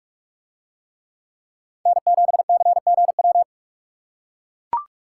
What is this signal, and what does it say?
Morse code, heard as a keyed tone, about 44 wpm, 700 Hz. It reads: N7CGW